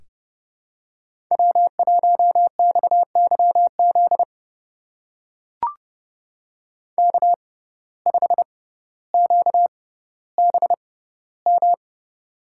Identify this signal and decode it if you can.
Morse code, heard as a keyed tone, about 28 wpm, spaced as for 15 wpm, 700 Hz. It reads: W1XYZ K5QBM
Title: USA Call Signs - 5 Characters - 6x Character Spacing - 30wpm